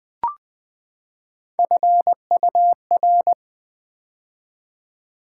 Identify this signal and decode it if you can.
Morse code, heard as a keyed tone, about 20 wpm, 700 Hz. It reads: FUR
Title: Build A Word - 3 Letters At A Time - 9-Letter Words - 20wpm